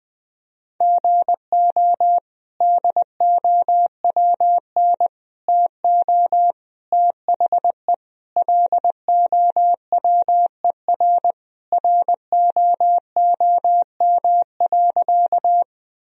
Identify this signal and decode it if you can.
Morse code, heard as a keyed tone, about 20 wpm, 700 Hz. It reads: GO DOWN TO THE LOWER ROOM.
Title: Sentences from Top 1000 Words - Review Part 4 - 20wpm